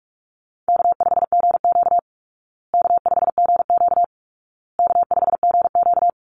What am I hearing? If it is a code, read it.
Morse code, heard as a keyed tone, about 44 wpm, 700 Hz. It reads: K5GQ K5GQ K5GQ
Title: Easy Call Signs Repeated 3x - 45wpm